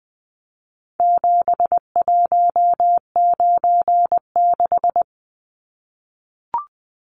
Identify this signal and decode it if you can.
Morse code, heard as a keyed tone, about 20 wpm, 700 Hz. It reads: 7196
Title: Numbers - 4-Digit 20wpm